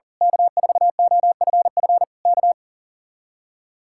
Morse code, heard as a keyed tone, about 38 wpm, 700 Hz. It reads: K4OFF K
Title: Speed-Racing Common QSO Phrases - 40wpm